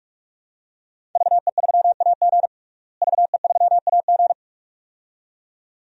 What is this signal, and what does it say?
Morse code, heard as a keyed tone, about 47 wpm, 700 Hz. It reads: VE3AG VE3AG
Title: DXCC-100 Call Signs - 5 Characters - Repeated 2x - 45wpm